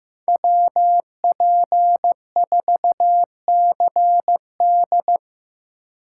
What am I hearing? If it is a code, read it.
Morse code, heard as a keyed tone, about 17 wpm, 700 Hz. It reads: WP4CD
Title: Speed Racing - USA 2x2 Call Signs 15wpm